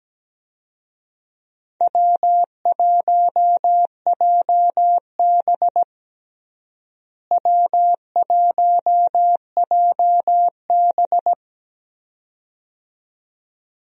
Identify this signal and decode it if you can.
Morse code, heard as a keyed tone, about 17 wpm, 700 Hz. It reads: W1JB W1JB
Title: USA Call Signs - 4 Characters - Repeated 2x - 17wpm